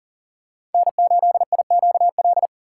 Morse code, heard as a keyed tone, about 40 wpm, 700 Hz. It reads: N8IQL
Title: USA Call Signs - 3x Character Spacing - 40wpm